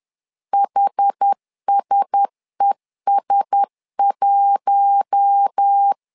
Morse code, heard as a keyed tone, about 10 wpm, 800 Hz. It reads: HSES1
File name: LEZIONI / 4 Q1S 10wpm 2133